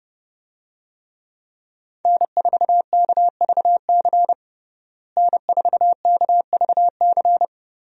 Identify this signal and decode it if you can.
Morse code, heard as a keyed tone, about 30 wpm, 700 Hz. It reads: N4KVC N4KVC
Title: USA Call Signs - 5 Characters - Repeated 3x - 30wpm